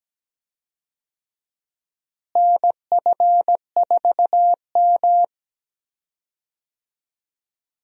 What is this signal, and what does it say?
Morse code, heard as a keyed tone, about 17 wpm, 700 Hz. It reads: NF4M
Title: USA Call Signs - 4 Characters - Repeated 2x - 17wpm